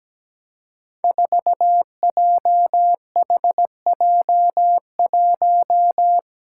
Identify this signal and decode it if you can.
Morse code, heard as a keyed tone, about 17 wpm, 700 Hz. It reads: 4JHJ1
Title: Five Challenging Letter-Number Combinations 17wpm